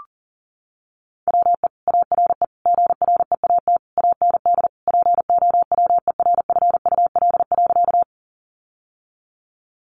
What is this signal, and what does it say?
Morse code, heard as a keyed tone, about 40 wpm, 700 Hz. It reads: WE ARE GREAT AND POWERFUL.
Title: Sentences from Top 1300 Words 40wpm